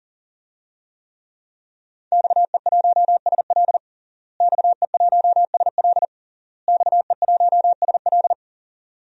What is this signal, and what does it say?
Morse code, heard as a keyed tone, about 40 wpm, 700 Hz. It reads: XE1SL XE1SL XE1SL
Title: DXCC-100 Call Signs - 5 Characters - Repeated 3x - 40wpm